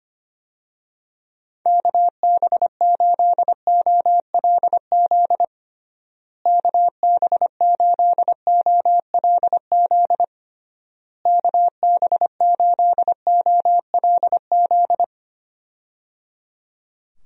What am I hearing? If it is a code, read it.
Morse code, heard as a keyed tone, about 25 wpm, 700 Hz. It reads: KB8OLZ KB8OLZ KB8OLZ
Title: USA Call Signs - 6 Characters - Repeated 3x - 25wpm